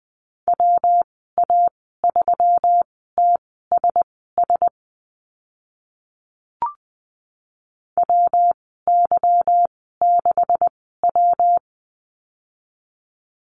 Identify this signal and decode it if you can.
Morse code, heard as a keyed tone, about 20 wpm, spaced as for 15 wpm, 700 Hz. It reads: WA3TSS WY6W
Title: 154 Lesson 25g - Letter p - Call Signs 20wpm